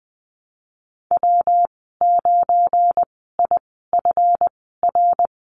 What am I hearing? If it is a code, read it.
Morse code, heard as a keyed tone, about 20 wpm, spaced as for 14 wpm, 700 Hz. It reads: W9IFR
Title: 245 Lesson 39g - Letter z - Call Signs 20wpm